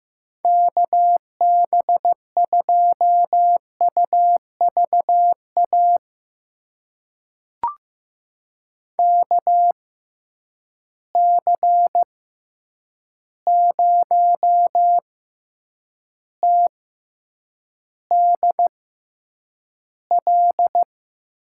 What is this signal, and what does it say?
Morse code, heard as a keyed tone, about 15 wpm, spaced as for 8 wpm, 700 Hz. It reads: KB2UVA KC0TDL
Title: USA Call Signs - 6 Characters - 6x Character Spacing - 15wpm